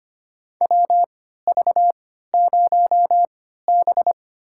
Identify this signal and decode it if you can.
Morse code, heard as a keyed tone, about 25 wpm, spaced as for 14 wpm, 700 Hz. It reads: WV0B
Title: USA Call Signs - 4 Characters - 3x Character Spacing - 25wpm